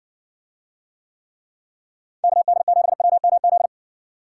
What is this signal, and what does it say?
Morse code, heard as a keyed tone, about 59 wpm, 700 Hz. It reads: KD7WMZ